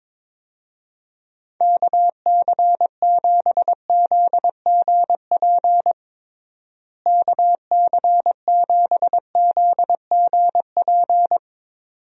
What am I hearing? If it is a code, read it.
Morse code, heard as a keyed tone, about 22 wpm, 700 Hz. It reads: KC7ZGP KC7ZGP
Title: USA Call Signs - 6 Characters - Repeated 2x - 22wpm